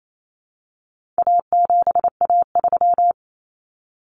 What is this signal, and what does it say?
Morse code, heard as a keyed tone, about 28 wpm, 700 Hz. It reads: A7A3
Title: Common Cut Numbers - 4-Digit 28wpm